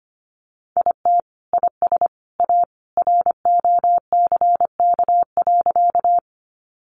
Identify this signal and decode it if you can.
Morse code, heard as a keyed tone, about 25 wpm, 700 Hz. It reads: IT IS A ROCK.
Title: Sentences from Top 400 Words 25wpm